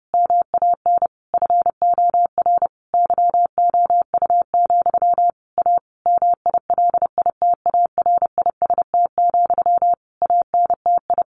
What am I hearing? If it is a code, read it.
Morse code, heard as a keyed tone, about 30 wpm, 700 Hz. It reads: MAN FOR YOU, A MILITARIST, ANTI
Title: A Sound Of Thunder 30wpm